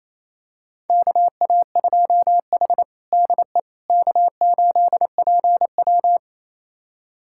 Morse code, heard as a keyed tone, about 27 wpm, 700 Hz. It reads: KA2H DE K8PW
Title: USA Callsign DE Callsign - 4 Characters - 28wpm